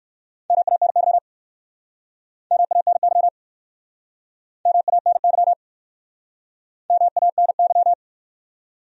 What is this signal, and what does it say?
Morse code, heard as a keyed tone, about 60 wpm, 700 Hz. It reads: MANY MANY MANY MANY